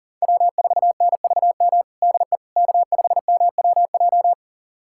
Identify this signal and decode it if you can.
Morse code, heard as a keyed tone, about 37 wpm, 700 Hz. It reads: W4NVM DE K5MWJ
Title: USA Callsign DE Callsign - 5 Characters - 40wpm